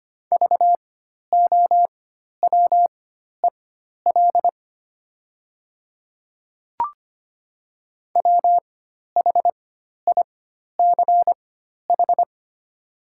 Morse code, heard as a keyed tone, about 25 wpm, spaced as for 12 wpm, 700 Hz. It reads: VOWEL WHICH
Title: 4x Character Spacing - Top 500 Words - 25wpm